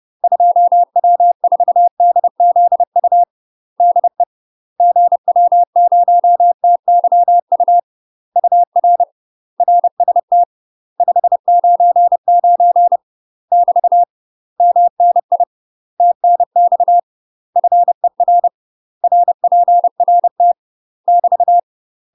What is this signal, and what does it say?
Morse code, heard as a keyed tone, about 30 wpm, 700 Hz. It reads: 2W4DZU DE GW0TYU UR RST 599 <BT> MNI TNX FER RPRT <BT>
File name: UK MASTER QSOs TRAINING / 30WPM / UK MASTER QSOs 0003